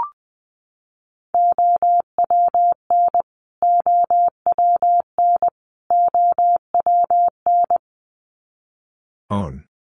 Morse code, heard as a keyed tone, about 20 wpm, 700 Hz. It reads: OWN OWN OWN
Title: Top 500 Words Repeated 20wpm